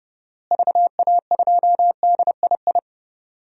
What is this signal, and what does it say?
Morse code, heard as a keyed tone, about 31 wpm, 700 Hz. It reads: VA2DII
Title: DXCC-100 Call Signs - 6 Characters - Repeated 2x - 30wpm